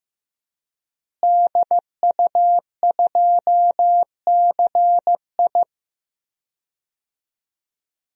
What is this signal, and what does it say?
Morse code, heard as a keyed tone, about 15 wpm, 700 Hz. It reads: DU2CI
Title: DXCC-100 Call Signs - 5 Characters - Repeated 3x - 15wpm